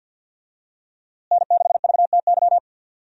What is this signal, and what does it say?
Morse code, heard as a keyed tone, about 52 wpm, 700 Hz. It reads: N6VTY
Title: USA Call Signs - 5 Characters - Repeated 3x - 50wpm